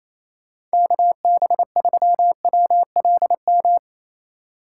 Morse code, heard as a keyed tone, about 34 wpm, 700 Hz. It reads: KB3WLM
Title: Speed Racing - USA 2x3 Call Signs 28wpm